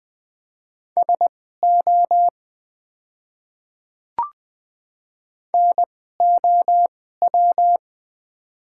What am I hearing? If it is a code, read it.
Morse code, heard as a keyed tone, about 20 wpm, spaced as for 14 wpm, 700 Hz. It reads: SO NOW
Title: 2x Character Spacing - Top 100 Words - 20wpm